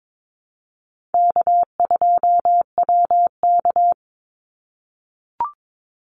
Morse code, heard as a keyed tone, about 22 wpm, 700 Hz. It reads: K2WK
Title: Call Signs - 22wpm